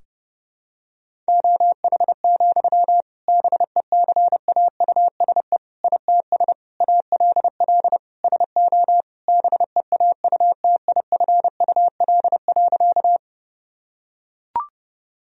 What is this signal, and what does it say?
Morse code, heard as a keyed tone, about 30 wpm, 700 Hz. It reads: OH, BECAUSE ITS ALL SO BEAUTIFUL.
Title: Sentences from Top 1000 Words - Review Part 3 - 30wpm